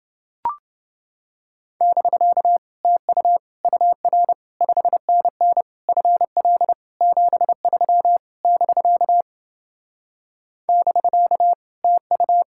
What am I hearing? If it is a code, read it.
Morse code, heard as a keyed tone, about 28 wpm, 700 Hz. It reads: <BK> TU UR 5NN FL 73 <BK> <BK> TU